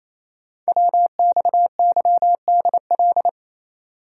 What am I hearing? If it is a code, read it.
Morse code, heard as a keyed tone, about 27 wpm, 700 Hz. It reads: WXYDL
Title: Code Groups - 5 Letters - 28wpm